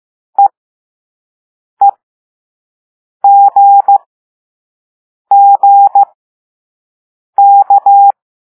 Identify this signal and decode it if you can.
Morse code, heard as a keyed tone, about 15 wpm, spaced as for 5 wpm, 800 Hz. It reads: EEGGK
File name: Morse / Level5-All-B